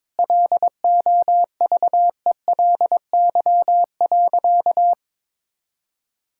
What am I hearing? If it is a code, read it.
Morse code, heard as a keyed tone, about 22 wpm, 700 Hz. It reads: LOVELY.
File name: practice / 22wpm / Sentences with 3-Word Phrases - 22wpm